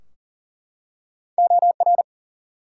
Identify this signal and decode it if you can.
Morse code, heard as a keyed tone, about 40 wpm, 700 Hz. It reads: OR
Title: Two Letter Words - 40wpm